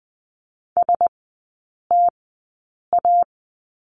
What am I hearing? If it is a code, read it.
Morse code, heard as a keyed tone, about 20 wpm, spaced as for 8 wpm, 700 Hz. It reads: STA
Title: 053 Lesson 10e - Letter r - Three Characters 20wpm